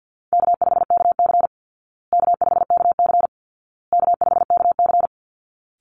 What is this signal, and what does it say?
Morse code, heard as a keyed tone, about 48 wpm, 700 Hz. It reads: K5KC K5KC K5KC
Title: Easy Call Signs Repeated 3x - 50wpm